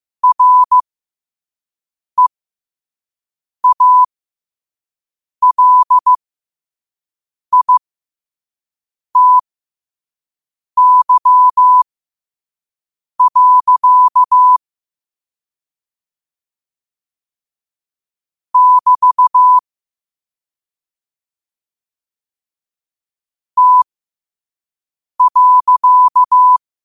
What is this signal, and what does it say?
Morse code, heard as a keyed tone, about 15 wpm, spaced as for 5 wpm, 1000 Hz. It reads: REALITY. <BT> T.
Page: Quotes for Thu, 14 Aug 2025 in Morse Code at 5 words per minute.